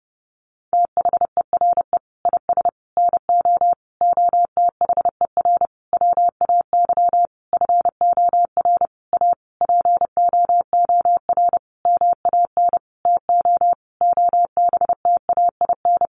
Morse code, heard as a keyed tone, about 30 wpm, 700 Hz. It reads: THERE IS NO OTHER WAY FOR A POOR MAN TO OBTAIN
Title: Sentences from Top 1300 Words 30wpm